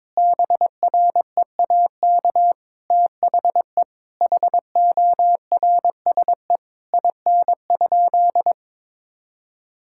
Morse code, heard as a keyed tone, about 21 wpm, 700 Hz. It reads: BREAK THE HORSE IN?
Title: Sentences from Top 1000 Words - Review Part 6 - 22wpm